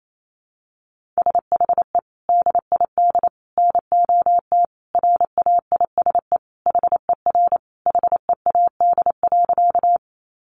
Morse code, heard as a keyed tone, about 28 wpm, 700 Hz. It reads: SHE DID NOT RAISE HER HEAD.
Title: Sentences from Top 1400 Words 28wpm